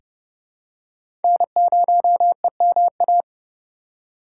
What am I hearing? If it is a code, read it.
Morse code, heard as a keyed tone, about 31 wpm, 700 Hz. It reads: N0EMA
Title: USA Call Signs - 5 Characters - 2x Character Spacing - 30wpm